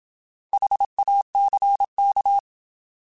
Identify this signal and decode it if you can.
Morse code, heard as a keyed tone, about 27 wpm, 800 Hz. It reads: HACK
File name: Alphabetical / Alphabetical mp3s / Hs @25